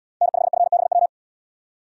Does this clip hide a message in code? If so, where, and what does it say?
Yes: Morse code from 0.2 to 1.1 s, heard as a keyed tone, about 77 wpm, 700 Hz. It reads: N5FBW